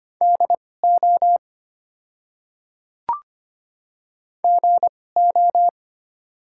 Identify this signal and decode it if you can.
Morse code, heard as a keyed tone, about 25 wpm, spaced as for 18 wpm, 690 Hz. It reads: DO GO
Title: 2x Character Spacing - Top 100 Words - 25wpm